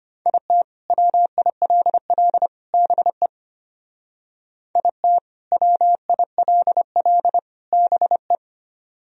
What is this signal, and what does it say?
Morse code, heard as a keyed tone, about 32 wpm, 700 Hz. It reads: IT WILL BE IT WILL BE